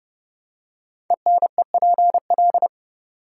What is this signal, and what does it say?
Morse code, heard as a keyed tone, about 30 wpm, 700 Hz. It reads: ENEPL
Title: Code Groups - 5 Letters - 30wpm